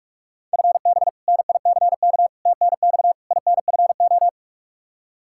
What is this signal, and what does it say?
Morse code, heard as a keyed tone, about 41 wpm, 700 Hz. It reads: FB NICK TNX INFO
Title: Practice QSO - Exchange 3 - More Info - 45wpm